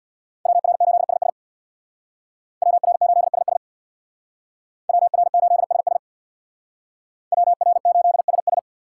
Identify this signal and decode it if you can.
Morse code, heard as a keyed tone, about 61 wpm, 700 Hz. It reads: WR8SS WR8SS WR8SS WR8SS
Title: Speed Racing - USA 2x2 Call Signs 50wpm